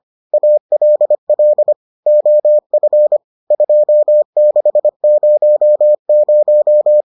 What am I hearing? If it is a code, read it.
Morse code, heard as a keyed tone, about 25 wpm, 600 Hz. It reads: ALL OF 2600